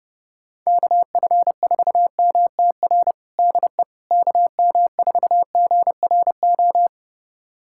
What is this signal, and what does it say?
Morse code, heard as a keyed tone, about 30 wpm, 700 Hz. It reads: KF4MTR DE KM4GRO
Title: USA Callsign DE Callsign 30wpm